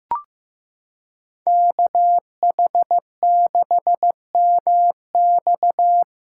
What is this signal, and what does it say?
Morse code, heard as a keyed tone, about 16 wpm, 700 Hz. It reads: KH6MX
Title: USA Callsign DE Callsign - 5 Characters - 15wpm